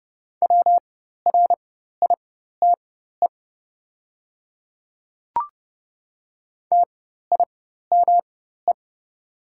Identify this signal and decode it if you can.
Morse code, heard as a keyed tone, about 30 wpm, spaced as for 14 wpm, 700 Hz. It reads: WRITE TIME
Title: 4x Character Spacing - Top 100 Words - 30wpm